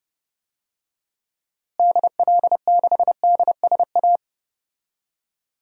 Morse code, heard as a keyed tone, about 32 wpm, 700 Hz. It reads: DL6DSA
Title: DXCC-100 Call Signs - 6 Characters - Repeated 2x - 30wpm